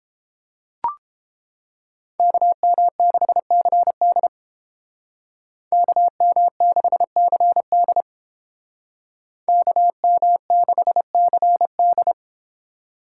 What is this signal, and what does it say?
Morse code, heard as a keyed tone, about 31 wpm, 700 Hz. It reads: KM6CD KM6CD KM6CD